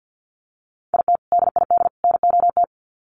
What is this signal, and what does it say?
Morse code, heard as a keyed tone, about 50 wpm, 700 Hz. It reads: IT DID NOT
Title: Sets of 3 Words 50wpm